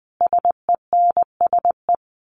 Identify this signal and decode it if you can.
Morse code, heard as a keyed tone, about 20 wpm, 700 Hz. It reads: SENSE
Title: Sentences from Top 1200 Words 20wpm